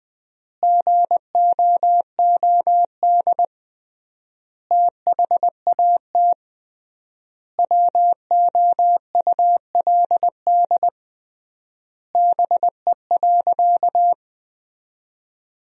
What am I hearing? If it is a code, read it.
Morse code, heard as a keyed tone, about 20 wpm, 700 Hz. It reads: GOOD THAT WOULD BE.
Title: 3x Word-Spacing - Sentences from Top 200 Words - 20wpm